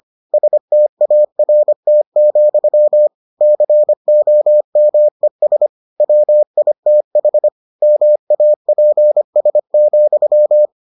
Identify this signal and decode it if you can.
Morse code, heard as a keyed tone, about 25 wpm, 600 Hz. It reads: START, COMES WITH MAPS,